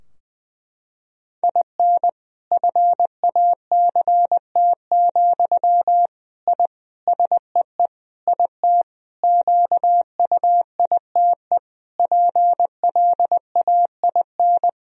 Morse code, heard as a keyed tone, about 21 wpm, 700 Hz. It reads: IN FACT, I SEE IT QUITE PLAIN
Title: Sentences from Top 1000 Words - Review Part 3 - 20wpm